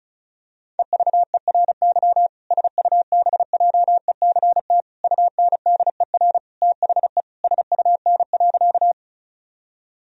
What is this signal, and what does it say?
Morse code, heard as a keyed tone, about 34 wpm, 700 Hz. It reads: EVERY SUBJECT UNDER THE SUN.
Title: Sentences from Top 1000 Words - Review Part 1 - 35wpm